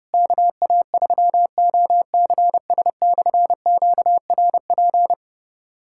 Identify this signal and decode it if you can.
Morse code, heard as a keyed tone, about 30 wpm, 700 Hz. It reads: KA3OCS/QRP
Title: K1USN SST QSO Practice - 30wpm